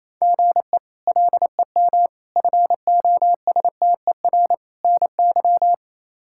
Morse code, heard as a keyed tone, about 27 wpm, 700 Hz. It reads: GE LEM FOSTER NY
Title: K1USN SST QSO Practice - 28wpm